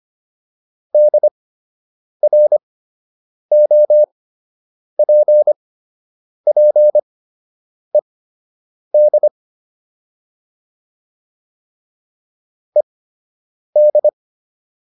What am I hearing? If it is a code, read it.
Morse code, heard as a keyed tone, about 25 wpm, spaced as for 8 wpm, 600 Hz. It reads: DROPPED ED